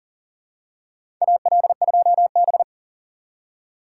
Morse code, heard as a keyed tone, about 40 wpm, 700 Hz. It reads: AL2B
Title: USA Call Signs - 4 Characters - 3x Character Spacing - 40wpm